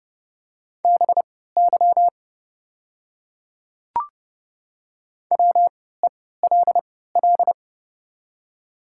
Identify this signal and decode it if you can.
Morse code, heard as a keyed tone, about 30 wpm, spaced as for 17 wpm, 700 Hz. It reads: BY WELL
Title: 3x Character Spacing - Top 300 Words - 30wpm